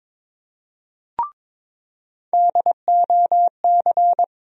Morse code, heard as a keyed tone, about 22 wpm, 700 Hz. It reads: DOC
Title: USA Call Sign Postfixes - 3 Letters 22wpm